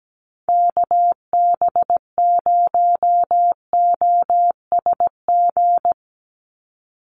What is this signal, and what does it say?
Morse code, heard as a keyed tone, about 17 wpm, 700 Hz. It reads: KB0OSG
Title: USA Call Signs 17wpm